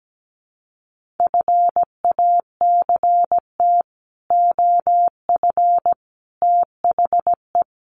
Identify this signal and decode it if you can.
Morse code, heard as a keyed tone, about 17 wpm, 700 Hz. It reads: FACT OF THE
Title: Sets of 5 Words 17wpm